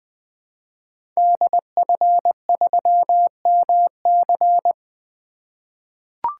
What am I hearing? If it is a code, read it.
Morse code, heard as a keyed tone, about 20 wpm, 700 Hz. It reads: DF3MC
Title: 2023 SOTA Activator Call Signs - 20wpm